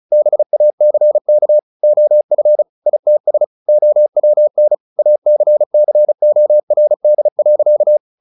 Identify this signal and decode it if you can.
Morse code, heard as a keyed tone, about 36 wpm, 600 Hz. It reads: BACK OF ITS OWN ACCORD.
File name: the old man and the sea / the old man and the sea 1 35wpm